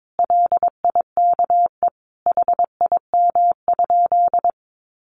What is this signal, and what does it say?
Morse code, heard as a keyed tone, about 22 wpm, 700 Hz. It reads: LIKE HIM?
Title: Sentences from Top 100 Words 22wpm